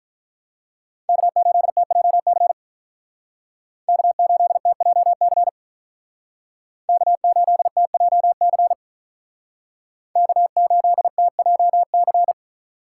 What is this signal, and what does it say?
Morse code, heard as a keyed tone, about 44 wpm, 700 Hz. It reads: K8TJC K8TJC K8TJC K8TJC